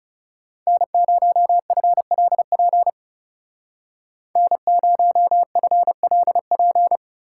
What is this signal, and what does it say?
Morse code, heard as a keyed tone, about 39 wpm, 700 Hz. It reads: N0FLP N0FLP